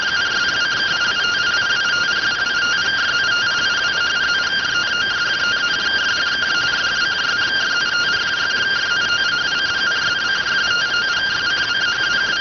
Two-tone teleprinter signal, about 170 Hz shift.